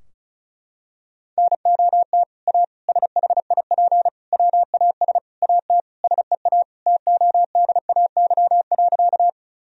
Morse code, heard as a keyed tone, about 36 wpm, 700 Hz. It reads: NOT A SHIP WAS AT SEA TODAY.
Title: Sentences from Top 1000 Words - Review Part 2 - 35wpm